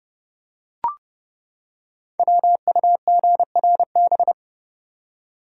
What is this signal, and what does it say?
Morse code, heard as a keyed tone, about 30 wpm, 700 Hz. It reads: WUGRB
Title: Code Groups - 5 Letters - 30wpm